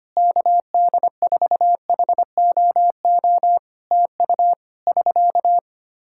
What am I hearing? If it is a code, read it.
Morse code, heard as a keyed tone, about 25 wpm, 700 Hz. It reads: KD4HOO TU <SK>
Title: Practice QSO - Exchange 4 - Ending - 25wpm